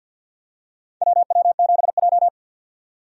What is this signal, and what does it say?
Morse code, heard as a keyed tone, about 53 wpm, 700 Hz. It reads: WW7J
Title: USA Call Signs - 4 Characters - 50wpm